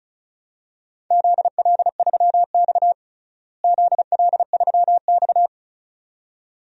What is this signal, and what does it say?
Morse code, heard as a keyed tone, about 34 wpm, 700 Hz. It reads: ZL3X ZL3X
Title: DXCC-100 Call Signs - 4 Characters - Repeated 2x - 35wpm